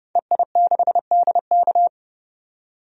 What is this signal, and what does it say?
Morse code, heard as a keyed tone, about 32 wpm, 700 Hz. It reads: EI6DK
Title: DXCC-100 Call Signs - 5 Characters - 30wpm